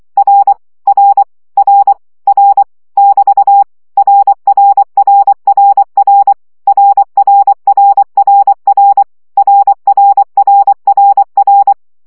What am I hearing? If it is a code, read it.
Morse code, heard as a keyed tone, about 24 wpm, 800 Hz. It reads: R R R R <BT> RRRRR RRRRR RRRRR
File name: CW gyakorlas r 20 baud